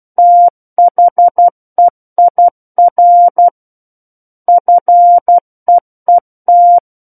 Morse code, heard as a keyed tone, about 12 wpm, 700 Hz. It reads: THEIR FEET
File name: Book One / 03-Chapter 3 - 12 WPM